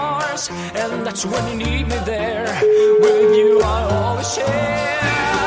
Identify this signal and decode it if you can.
Ringback tone (Europe, 425 Hz), with noise.